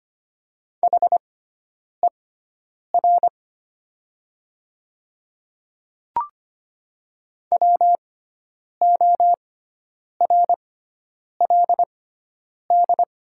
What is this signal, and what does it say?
Morse code, heard as a keyed tone, about 25 wpm, spaced as for 9 wpm, 700 Hz. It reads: HER WORLD
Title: 6x Character Spacing - Top 300 Words - 25wpm